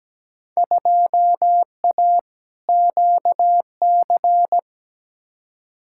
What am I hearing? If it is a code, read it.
Morse code, heard as a keyed tone, about 17 wpm, 700 Hz. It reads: 2A QC
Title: Field Day Exchanges - 17wpm